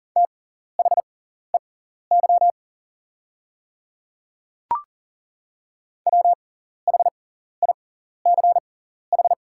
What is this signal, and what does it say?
Morse code, heard as a keyed tone, about 40 wpm, spaced as for 14 wpm, 690 Hz. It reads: THEY WHICH
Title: 6x Character Spacing - Top 200 Words - 40wpm